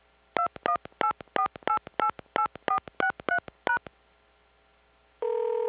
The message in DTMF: 2184888463#